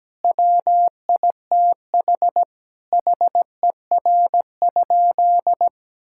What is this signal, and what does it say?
Morse code, heard as a keyed tone, about 17 wpm, 700 Hz. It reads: WITH HER?
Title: Sentences from Top 1000 Words - Review Part 6 - 17wpm